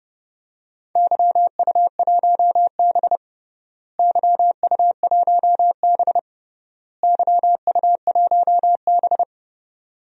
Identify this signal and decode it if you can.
Morse code, heard as a keyed tone, about 30 wpm, 700 Hz. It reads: YU1B YU1B YU1B
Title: DXCC-100 Call Signs - 4 Characters - Repeated 3x - 30wpm